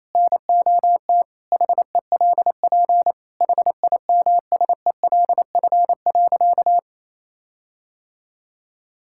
Morse code, heard as a keyed tone, about 28 wpm, 700 Hz. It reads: NOT HELP HIMSELF.
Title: Sentences from Top 1600 Words 28wpm